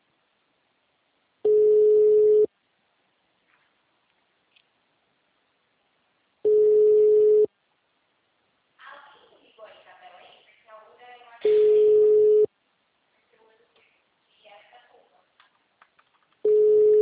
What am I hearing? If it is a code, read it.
Ringback tone (Europe, 425 Hz), with speech.
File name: out- -2011- - - .1041